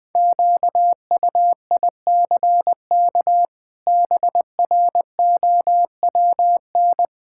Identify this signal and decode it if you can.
Morse code, heard as a keyed tone, about 20 wpm, 700 Hz. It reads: QUICK BROWN
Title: Sets of 8 Words - 20wpm